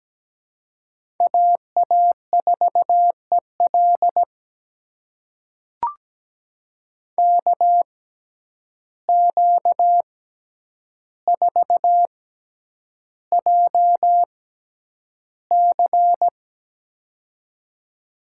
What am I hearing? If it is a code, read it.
Morse code, heard as a keyed tone, about 17 wpm, spaced as for 9 wpm, 700 Hz. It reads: AA4EL KQ4JC
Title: USA Call Signs - 5 Characters - 6x Character Spacing - 17wpm